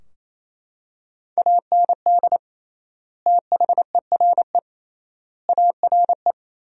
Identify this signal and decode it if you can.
Morse code, heard as a keyed tone, about 28 wpm, 700 Hz. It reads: AND THERE ARE
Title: 3x Word-Spacing - Sentences from Top 300 Words - 28wpm